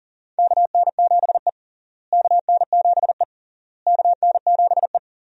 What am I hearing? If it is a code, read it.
Morse code, heard as a keyed tone, about 40 wpm, 700 Hz. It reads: KN7E KN7E KN7E
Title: USA Call Signs - 4 Characters - Repeated 3x - 40wpm